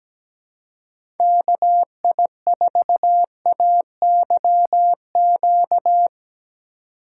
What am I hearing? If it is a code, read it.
Morse code, heard as a keyed tone, about 20 wpm, 700 Hz. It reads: KI4AYQ
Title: Speed Racing - USA Call Signs 17wpm